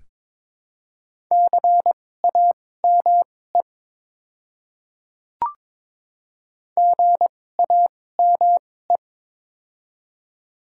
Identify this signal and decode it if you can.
Morse code, heard as a keyed tone, about 22 wpm, spaced as for 16 wpm, 700 Hz. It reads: CAME GAME
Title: 2x Character Spacing - Top 500 Words - 22wpm